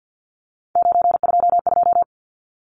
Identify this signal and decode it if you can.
Morse code, heard as a keyed tone, about 50 wpm, 700 Hz. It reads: 922
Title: Numbers - 3-Digit 50wpm